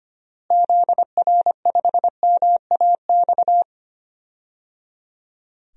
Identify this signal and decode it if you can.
Morse code, heard as a keyed tone, about 25 wpm, 700 Hz. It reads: ZR5MAX
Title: DXCC-100 Call Signs - 6 Characters - Repeated 3x - 25wpm